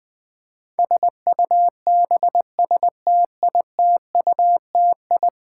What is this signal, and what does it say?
Morse code, heard as a keyed tone, about 20 wpm, 700 Hz. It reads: SUBSTITUTI
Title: Build A Word - 5 Letters At A Time - 12-Letter Words - 20wpm